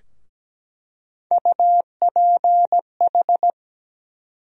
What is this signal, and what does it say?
Morse code, heard as a keyed tone, about 17 wpm, 700 Hz. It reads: UPH
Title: USA Call Sign Postfixes - 3 Letters 17wpm